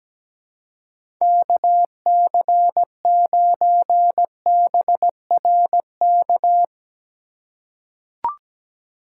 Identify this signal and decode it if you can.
Morse code, heard as a keyed tone, about 17 wpm, 700 Hz. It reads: KC9BRK
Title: USA Call Signs - 6 Characters - 4x Character Spacing - 17wpm